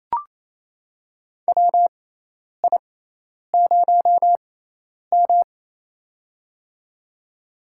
Morse code, heard as a keyed tone, about 27 wpm, spaced as for 10 wpm, 700 Hz. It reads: WI0M
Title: USA Call Signs - 4 Characters - 6x Character Spacing - 28wpm